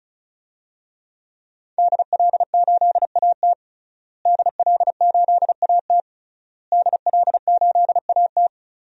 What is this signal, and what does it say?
Morse code, heard as a keyed tone, about 34 wpm, 700 Hz. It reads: DL8AT DL8AT DL8AT
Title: DXCC-100 Call Signs - 5 Characters - Repeated 3x - 35wpm